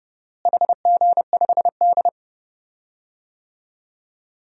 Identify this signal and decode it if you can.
Morse code, heard as a keyed tone, about 30 wpm, 700 Hz. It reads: HG5D
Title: DXCC-100 Call Signs - 4 Characters - Repeated 2x - 30wpm